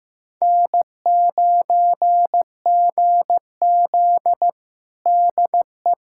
Morse code, heard as a keyed tone, about 15 wpm, 700 Hz. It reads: N9GZ DE
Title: USA Callsign DE Callsign - 4 Characters - 15wpm